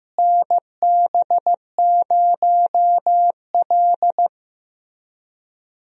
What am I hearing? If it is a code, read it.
Morse code, heard as a keyed tone, about 15 wpm, 700 Hz. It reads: NB0L
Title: USA Call Signs - 4 Characters - Repeated 2x - 15wpm